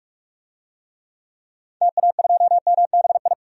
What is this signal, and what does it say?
Morse code, heard as a keyed tone, about 43 wpm, 700 Hz. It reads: TA2MBI
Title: DXCC-100 Call Signs - 6 Characters - 45wpm